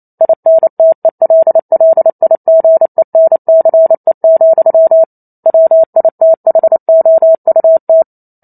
Morse code, heard as a keyed tone, about 28 wpm, 650 Hz. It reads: INTELLIGENCE, WITHOUT